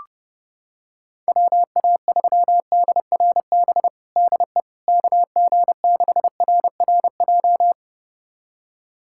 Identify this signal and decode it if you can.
Morse code, heard as a keyed tone, about 30 wpm, 700 Hz. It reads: WA3DRB DE KG6RRJ
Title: USA Callsign DE Callsign - 6 Characters - 30wpm